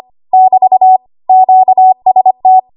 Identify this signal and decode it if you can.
Morse code, heard as a keyed tone, about 25 wpm, 750 Hz. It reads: <BT> QST
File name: Morse / 25WPM